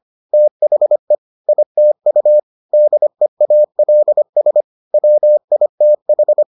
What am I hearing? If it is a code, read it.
Morse code, heard as a keyed tone, about 25 wpm, 600 Hz. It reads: THE ITU DEALS WITH